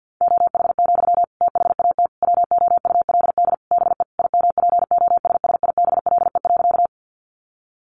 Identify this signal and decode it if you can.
Morse code, heard as a keyed tone, about 48 wpm, 700 Hz. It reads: OH, THAT WOULD BE IMPOSSIBLE.
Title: Sentences from Top 1100 Words 50wpm